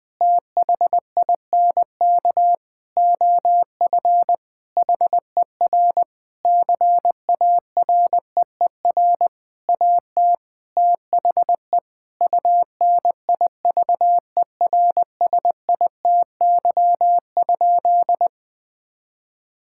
Morse code, heard as a keyed tone, about 20 wpm, 700 Hz. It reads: THINK OF HER CAREER AT THE UNIVERSITY?
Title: Sentences from Top 1000 Words - Review Part 6 - 20wpm